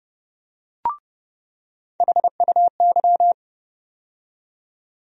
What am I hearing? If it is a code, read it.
Morse code, heard as a keyed tone, about 30 wpm, 700 Hz. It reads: HUY
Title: USA Call Sign Postfixes - 3 Letters 30wpm